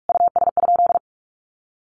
Morse code, heard as a keyed tone, about 44 wpm, 700 Hz. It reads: US?
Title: Sentences from Top 1500 Words 45wpm